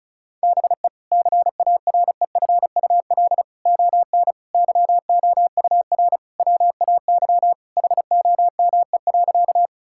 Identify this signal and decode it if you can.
Morse code, heard as a keyed tone, about 36 wpm, 700 Hz. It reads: BE CAREFUL ON YOUR WAY HOME.
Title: Sentences from Top 1600 Words 35wpm